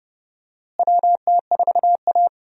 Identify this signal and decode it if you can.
Morse code, heard as a keyed tone, about 31 wpm, 700 Hz. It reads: WT4A
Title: USA Call Signs - 4 Characters - Repeated 3x - 30wpm